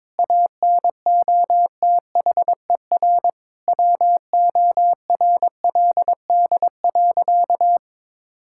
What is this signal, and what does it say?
Morse code, heard as a keyed tone, about 22 wpm, 700 Hz. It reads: ANOTHER WORLD.
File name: Difficult Sentences with Hints - 22wpm